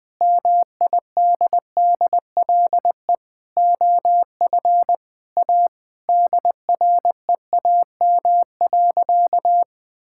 Morse code, heard as a keyed tone, about 21 wpm, 700 Hz. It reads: MIDDLE OF A DREAM.
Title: Sentences with 4-Word Phrases - 20wpm